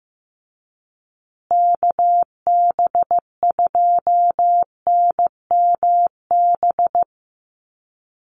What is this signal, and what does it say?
Morse code, heard as a keyed tone, about 15 wpm, 700 Hz. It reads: KB2NMB
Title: USA Call Signs Repeated 2x 15wpm